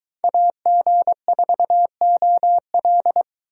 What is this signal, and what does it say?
Morse code, heard as a keyed tone, about 23 wpm, 700 Hz. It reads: AG4OL